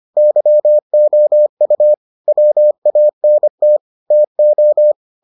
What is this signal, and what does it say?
Morse code, heard as a keyed tone, about 25 wpm, 600 Hz. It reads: YOU WANT TO